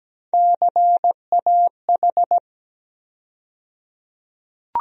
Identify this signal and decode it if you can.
Morse code, heard as a keyed tone, about 17 wpm, 700 Hz. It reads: CAH
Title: USA Call Sign Postfixes - 3 Letters 17wpm